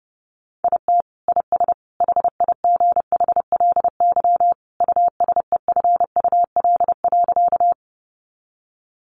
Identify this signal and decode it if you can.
Morse code, heard as a keyed tone, about 28 wpm, 700 Hz. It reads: IT IS HIGHLY USEFUL.
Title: Sentences from Top 1100 Words 30wpm